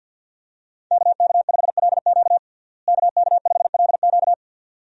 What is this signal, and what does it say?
Morse code, heard as a keyed tone, about 46 wpm, 700 Hz. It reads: KK5LQ KK5LQ
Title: USA Call Signs - 5 Characters - Repeated 2x - 50wpm